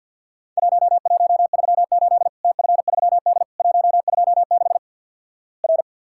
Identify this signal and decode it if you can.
Morse code, heard as a keyed tone, about 48 wpm, 700 Hz. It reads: 1138 TF3D 126 R
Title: Worked All Europe DX Contest - QTC Exchange - 50wpm